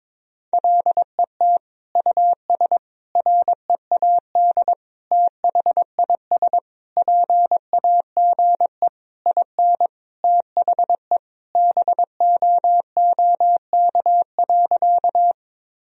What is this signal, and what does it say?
Morse code, heard as a keyed tone, about 22 wpm, 700 Hz. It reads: LET US READ THIS PAGE IN THE BOOK.
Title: Sentences from Top 1000 Words - Review Part 1 - 22wpm